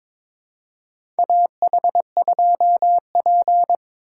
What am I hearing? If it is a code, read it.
Morse code, heard as a keyed tone, about 22 wpm, 700 Hz. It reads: AH2P
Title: USA Call Signs - 4 Characters - 4x Character Spacing - 22wpm